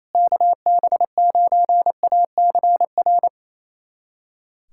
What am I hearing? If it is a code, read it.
Morse code, heard as a keyed tone, about 28 wpm, 700 Hz. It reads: KB9ACR
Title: USA Callsign DE Callsign - 6 Characters - 28wpm